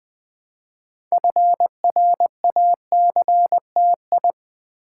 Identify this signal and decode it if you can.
Morse code, heard as a keyed tone, about 20 wpm, 700 Hz. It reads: FRACTI
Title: Build A Word - 2 Letters At A Time - 8-Letter Words - 20wpm